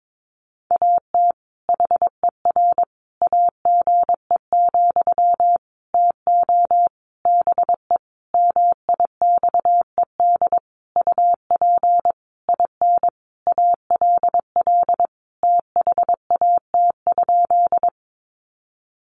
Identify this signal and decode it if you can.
Morse code, heard as a keyed tone, about 22 wpm, 700 Hz. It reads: AT HER AGE, TO BE MIXED UP IN ALL THAT?
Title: Sentences from Top 1500 Words 22wpm